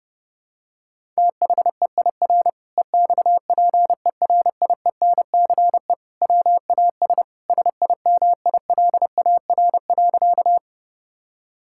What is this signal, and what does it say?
Morse code, heard as a keyed tone, about 30 wpm, 700 Hz. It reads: THEIR EXPERIENCE WAS SIMILAR.
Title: Sentences from Top 1000 Words - Review Part 5 - 30wpm